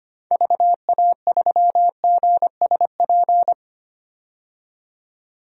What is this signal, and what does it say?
Morse code, heard as a keyed tone, about 25 wpm, 700 Hz. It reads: VA3GSP
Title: DXCC-100 Call Signs - 6 Characters - 25wpm